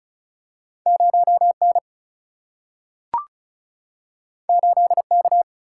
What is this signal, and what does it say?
Morse code, heard as a keyed tone, about 36 wpm, 700 Hz. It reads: 0N 8K
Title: Single-Letter Single-Number 35wpm